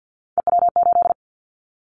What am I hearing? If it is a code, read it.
Morse code, heard as a keyed tone, about 50 wpm, 700 Hz. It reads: EW8
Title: Callsign Prefixes for Top 50 DXCC - 50wpm